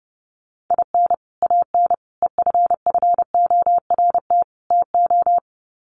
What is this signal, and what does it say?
Morse code, heard as a keyed tone, about 30 wpm, 700 Hz. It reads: IN AN EFFORT TO
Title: Sets of 4 Words 30wpm